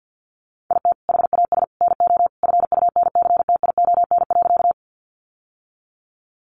Morse code, heard as a keyed tone, about 50 wpm, 700 Hz. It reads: IT HAS NO FUNCTION.
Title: Sentences from Top 1100 Words 50wpm